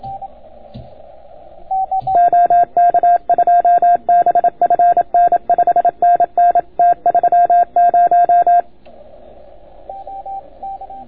Morse code, heard as a keyed tone, about 29 wpm, 690 Hz, with radio noise. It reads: OK2BFN 5NN T30